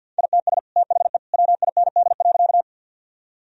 Morse code, heard as a keyed tone, about 50 wpm, 700 Hz. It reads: ITS THE WIND.
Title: Sentences from Top 1000 Words - Review Part 2 - 50wpm